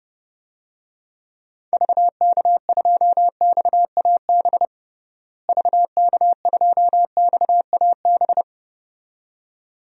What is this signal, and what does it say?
Morse code, heard as a keyed tone, about 30 wpm, 700 Hz. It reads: VK2XAB VK2XAB
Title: DXCC-100 Call Signs - 6 Characters - Repeated 2x - 30wpm